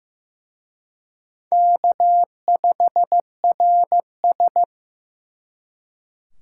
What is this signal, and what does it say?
Morse code, heard as a keyed tone, about 18 wpm, 700 Hz. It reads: K5RS
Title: Speed Racing - USA Call Signs 15wpm